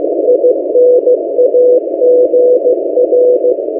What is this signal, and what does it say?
Morse code, heard as a keyed tone, about 15 wpm, 500 Hz, with noise. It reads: INAGR
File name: M08a- -1355utc-8096khz